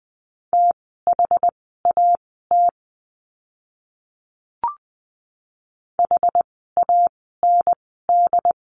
Morse code, heard as a keyed tone, about 20 wpm, spaced as for 15 wpm, 700 Hz. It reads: THAT HAND
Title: 115 Lesson 19f - Letter w - Words 20wpm